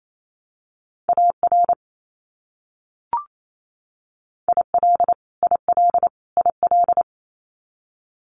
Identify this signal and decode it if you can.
Morse code, heard as a keyed tone, about 28 wpm, 700 Hz. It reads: AR IL IL IL
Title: US States Repeated 3x 28wpm